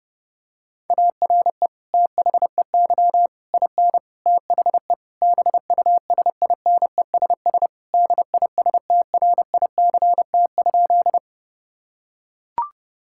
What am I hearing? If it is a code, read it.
Morse code, heard as a keyed tone, about 30 wpm, 700 Hz. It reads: ARE THEY IN THE BUSINESS DISTRICT?
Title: Sentences from Top 1600 Words 30wpm